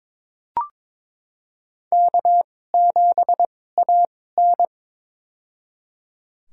Morse code, heard as a keyed tone, about 22 wpm, spaced as for 16 wpm, 700 Hz. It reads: K7AN
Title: USA Call Signs - 2x Character Spacing - 22wpm